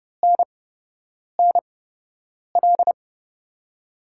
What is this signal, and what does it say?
Morse code, heard as a keyed tone, about 30 wpm, spaced as for 8 wpm, 700 Hz. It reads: NNL
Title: USA Call Signs - 6 Characters - 8x Character Spacing - 30wpm